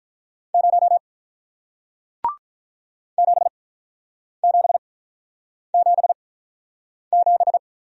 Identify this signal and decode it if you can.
Morse code, heard as a keyed tone, about 43 wpm, spaced as for 8 wpm, 690 Hz. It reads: 0 7777
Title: Speed-Racing Single Letter-Number 35wpm